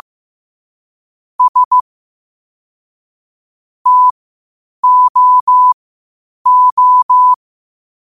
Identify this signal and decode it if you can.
Morse code, heard as a keyed tone, about 15 wpm, spaced as for 8 wpm, 1000 Hz. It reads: S TOO
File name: qotdfiles / 8wpm